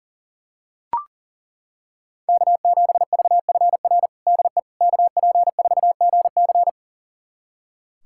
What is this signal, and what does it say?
Morse code, heard as a keyed tone, about 40 wpm, 700 Hz. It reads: K7VFR DE KP4GC
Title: USA Callsign DE Callsign - 5 Characters - 40wpm